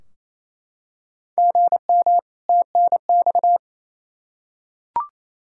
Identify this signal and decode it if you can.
Morse code, heard as a keyed tone, about 28 wpm, 700 Hz. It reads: GM TNX
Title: Practice QSO - Exchange 1 - Beginning - 28wpm